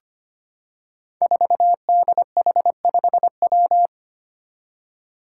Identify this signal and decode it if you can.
Morse code, heard as a keyed tone, about 25 wpm, 700 Hz. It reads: 4DH5W
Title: Five Challenging Letter-Number Combinations 25wpm